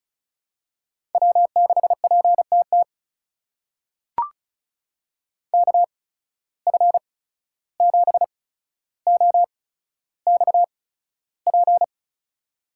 Morse code, heard as a keyed tone, about 36 wpm, spaced as for 14 wpm, 700 Hz. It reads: W6PTT KF7OXP
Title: USA Call Signs - 8x Character Spacing - 35wpm